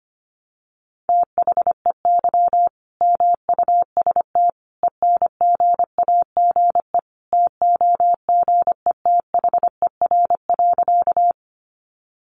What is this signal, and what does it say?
Morse code, heard as a keyed tone, about 25 wpm, 700 Hz. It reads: THEY MUST ENGAGE TOGETHER.
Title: Sentences from Top 1500 Words 25wpm